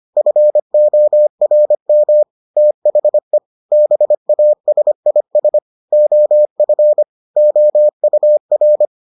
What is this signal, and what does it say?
Morse code, heard as a keyed tone, about 25 wpm, 600 Hz. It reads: FORM THE BASIS OF OUR